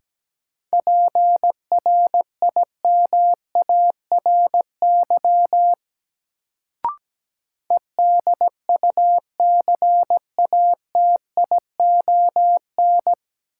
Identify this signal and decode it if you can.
Morse code, heard as a keyed tone, about 17 wpm, 700 Hz. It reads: PRIMARY EDUCATION
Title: Sentences from Top 1100 Words - New Word Drill Alt - 17wpm